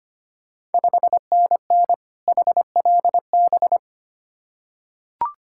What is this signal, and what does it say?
Morse code, heard as a keyed tone, about 25 wpm, 690 Hz. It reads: 5NN HLB
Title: Ontario QSO Party Exchange 25wpm